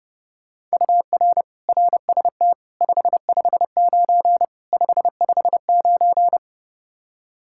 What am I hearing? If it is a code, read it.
Morse code, heard as a keyed tone, about 29 wpm, 700 Hz. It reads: UR RST 559 559
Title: Practice QSO - Exchange 1 - Beginning - 30wpm